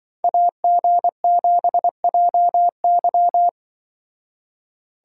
Morse code, heard as a keyed tone, about 25 wpm, 700 Hz. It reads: AG7JY
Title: Speed Racing - USA 2x2 Call Signs 20wpm